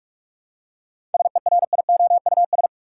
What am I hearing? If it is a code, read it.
Morse code, heard as a keyed tone, about 42 wpm, 700 Hz. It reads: SERIOUS
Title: Seven Letter Words - 45wpm